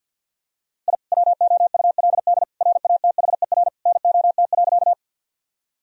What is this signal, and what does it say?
Morse code, heard as a keyed tone, about 50 wpm, 700 Hz. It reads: I WOULD RATHER NOT.
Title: Sentences from Top 1000 Words - Review Part 4 - 50wpm